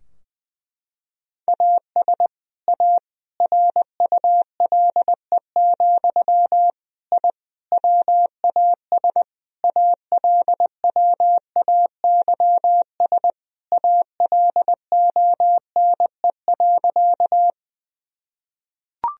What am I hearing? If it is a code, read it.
Morse code, heard as a keyed tone, about 20 wpm, 700 Hz. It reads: AS A RULE, I WAS ALWAYS ALONE.
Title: Sentences from Top 1000 Words - Review Part 2 - 20wpm